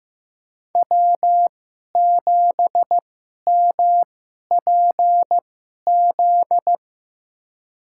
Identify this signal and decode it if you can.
Morse code, heard as a keyed tone, about 15 wpm, spaced as for 11 wpm, 700 Hz. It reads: W7MPZ
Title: USA Call Signs - 5 Characters - 2x Character Spacing - 15wpm